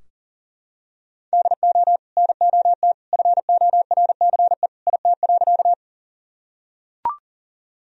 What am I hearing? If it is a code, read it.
Morse code, heard as a keyed tone, about 37 wpm, 700 Hz. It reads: DO NOT FORCE IT.
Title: Sentences from Top 1000 Words - Review Part 3 - 40wpm